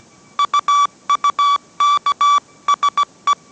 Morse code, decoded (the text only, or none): UUKSE